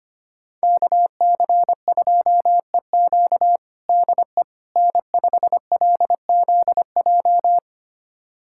Morse code, heard as a keyed tone, about 25 wpm, 700 Hz. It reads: KC2EQ DE N5LZJ
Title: USA Callsign DE Callsign - 5 Characters - 25wpm